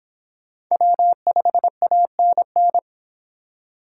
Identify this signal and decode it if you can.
Morse code, heard as a keyed tone, about 23 wpm, 700 Hz. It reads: W5ANN